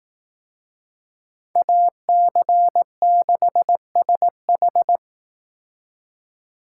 Morse code, heard as a keyed tone, about 18 wpm, 700 Hz. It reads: AC6SH
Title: Speed Racing - USA 2x2 Call Signs 15wpm